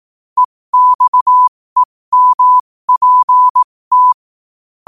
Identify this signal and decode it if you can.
Morse code, heard as a keyed tone, about 18 wpm, spaced as for 15 wpm, 1000 Hz. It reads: EXEMPT
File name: qotdfiles / 15wpm